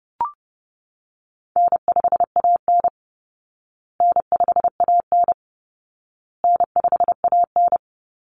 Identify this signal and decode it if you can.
Morse code, heard as a keyed tone, about 28 wpm, 700 Hz. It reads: N5AN N5AN N5AN
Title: Easy Call Signs Repeated 3x - 30wpm